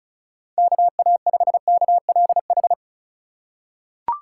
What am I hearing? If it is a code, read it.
Morse code, heard as a keyed tone, about 34 wpm, 700 Hz. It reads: KA5KLH
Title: USA Call Signs - 4x Character Spacing - 35wpm